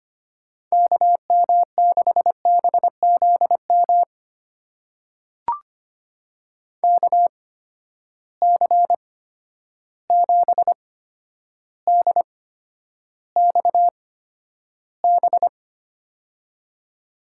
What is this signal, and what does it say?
Morse code, heard as a keyed tone, about 25 wpm, spaced as for 11 wpm, 700 Hz. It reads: KM6BZM KC7DXB
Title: USA Call Signs - 6 Characters - 8x Character Spacing - 25wpm